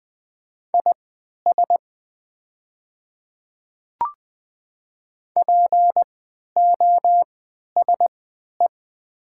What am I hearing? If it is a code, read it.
Morse code, heard as a keyed tone, about 20 wpm, spaced as for 11 wpm, 700 Hz. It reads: IS POSE
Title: 3x Character Spacing - Top 500 Words - 20wpm